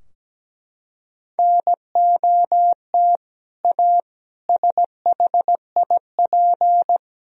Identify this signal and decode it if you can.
Morse code, heard as a keyed tone, about 17 wpm, 700 Hz. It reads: NOT A SHIP
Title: Sentences from Top 1000 Words - Review Part 2 - 17wpm